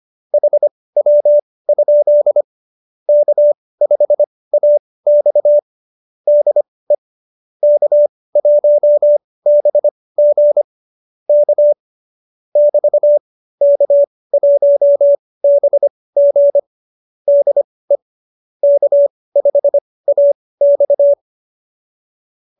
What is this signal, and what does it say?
Morse code, heard as a keyed tone, about 25 wpm, spaced as for 18 wpm, 590 Hz. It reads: HW? K5AX DE K1BG K <BT> K1BG DE K5AX
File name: qso201b 18